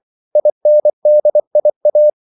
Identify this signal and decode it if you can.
Morse code, heard as a keyed tone, about 24 wpm, 600 Hz. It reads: INDIA
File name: CW / Practice / NationalCapitals24